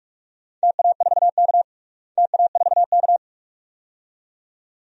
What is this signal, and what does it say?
Morse code, heard as a keyed tone, about 46 wpm, 700 Hz. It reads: TA4K TA4K
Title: DXCC-100 Call Signs - 4 Characters - Repeated 2x - 45wpm